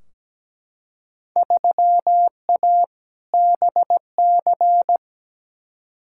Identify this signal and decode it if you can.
Morse code, heard as a keyed tone, about 17 wpm, 700 Hz. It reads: 3A BC
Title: Field Day Exchanges - 17wpm